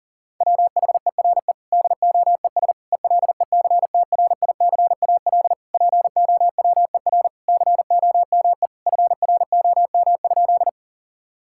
Morse code, heard as a keyed tone, about 40 wpm, 700 Hz. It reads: WHERE DOES ELECTRICAL POWER COME FROM?
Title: Sentences from Top 1000 Words - Review Part 6 - 40wpm